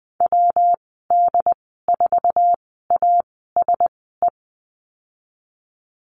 Morse code, heard as a keyed tone, about 20 wpm, spaced as for 15 wpm, 700 Hz. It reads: WD4ASE
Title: 140 Lesson 23g - Letter f - Call Signs 20wpm